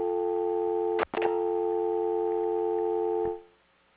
Dial tone (North America), 350 + 440 Hz, with noise.